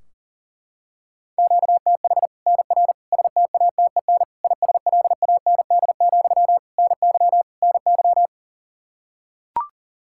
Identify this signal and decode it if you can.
Morse code, heard as a keyed tone, about 37 wpm, 700 Hz. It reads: QTH NR STATEN ISLAND, NY NY
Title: Practice QSO - Exchange 1 - Beginning - 40wpm